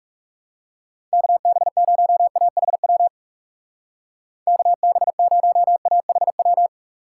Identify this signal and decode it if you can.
Morse code, heard as a keyed tone, about 37 wpm, 700 Hz. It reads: KB0AHW KB0AHW
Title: Speed Racing - USA Call Signs 30wpm